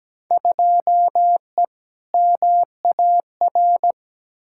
Morse code, heard as a keyed tone, about 17 wpm, 700 Hz. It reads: 2E MAR
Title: Field Day Exchanges Encore - 17wpm